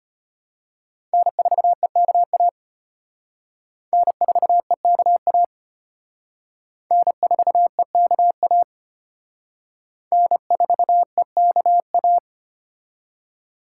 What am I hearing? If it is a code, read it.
Morse code, heard as a keyed tone, about 31 wpm, 700 Hz. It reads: N4EKA N4EKA N4EKA N4EKA
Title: Speed Racing - USA 1x3 Call Signs 25wpm